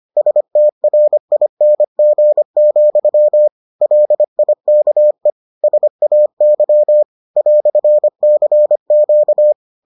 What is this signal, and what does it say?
Morse code, heard as a keyed tone, about 25 wpm, 600 Hz. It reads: STRING, LIKE SAY "CQ